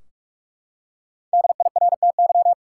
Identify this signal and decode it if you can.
Morse code, heard as a keyed tone, about 42 wpm, 700 Hz. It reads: DIRTY
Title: Five Letter Words - 45wpm